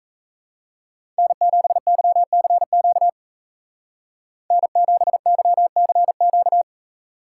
Morse code, heard as a keyed tone, about 34 wpm, 700 Hz. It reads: N7YCQ N7YCQ
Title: Speed Racing - USA 1x3 Call Signs 28wpm